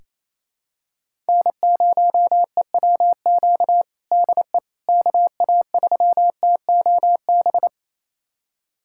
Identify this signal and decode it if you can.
Morse code, heard as a keyed tone, about 28 wpm, 700 Hz. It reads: N0EWQ DE KA3TOB
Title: USA Callsign DE Callsign 28wpm